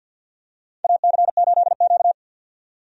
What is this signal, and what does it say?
Morse code, heard as a keyed tone, about 51 wpm, 700 Hz. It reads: AC8Q
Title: USA Call Signs - 4 Characters - 50wpm